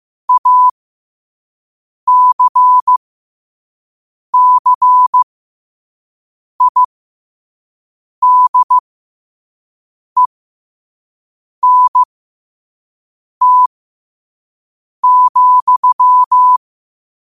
Morse code, heard as a keyed tone, about 15 wpm, spaced as for 5 wpm, 1000 Hz. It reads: ACCIDENT,
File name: 5wpm